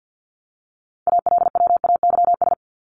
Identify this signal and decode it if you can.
Morse code, heard as a keyed tone, about 50 wpm, 700 Hz. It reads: ALWAYS
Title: Top 500 Words 50wpm